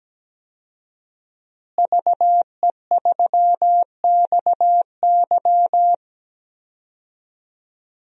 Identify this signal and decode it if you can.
Morse code, heard as a keyed tone, about 17 wpm, 700 Hz. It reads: VE3XY
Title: DXCC-100 Call Signs - 5 Characters - Repeated 3x - 17wpm